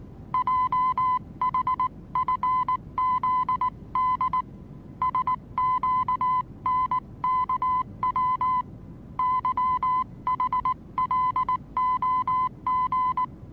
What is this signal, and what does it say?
Morse code, heard as a keyed tone, about 19 wpm, 1010 Hz. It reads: JHFZD SQNKW YHLOG
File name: repos / Morsekurs / DJ4UF Morsen26